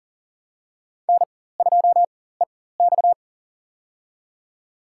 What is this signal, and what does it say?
Morse code, heard as a keyed tone, about 40 wpm, spaced as for 19 wpm, 700 Hz. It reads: N2EX
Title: USA Call Signs - 4x Character Spacing - 40wpm